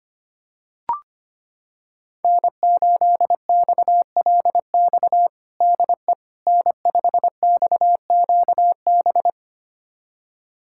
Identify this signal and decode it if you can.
Morse code, heard as a keyed tone, about 25 wpm, 700 Hz. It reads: N8XLX DE N5XQB
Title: USA Callsign DE Callsign - 5 Characters - 25wpm